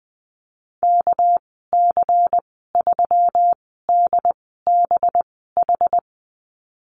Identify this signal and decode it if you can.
Morse code, heard as a keyed tone, about 20 wpm, spaced as for 15 wpm, 700 Hz. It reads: KC3DBH
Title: 207 Lesson 33g - Letter j - Call Signs 20wpm